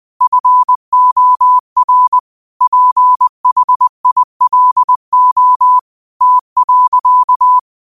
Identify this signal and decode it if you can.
Morse code, heard as a keyed tone, about 20 wpm, 1000 Hz. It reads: FOR PHILO T.
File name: 20wpm